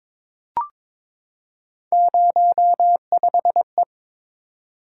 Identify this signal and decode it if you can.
Morse code, heard as a keyed tone, about 22 wpm, 700 Hz. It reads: 05E
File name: practice / 22wpm / Three Letter-Number Combinations - 22wpm